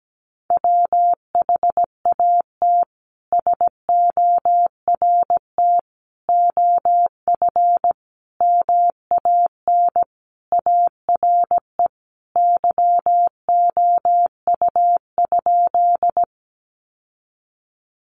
Morse code, heard as a keyed tone, about 17 wpm, 700 Hz. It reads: WHAT SORT OF MAN ARE YOU?
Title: Sentences from Top 1300 Words 17wpm